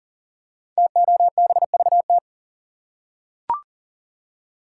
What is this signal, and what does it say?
Morse code, heard as a keyed tone, about 41 wpm, 700 Hz. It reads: TOBVT
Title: Code Groups - 5 Letters - 40wpm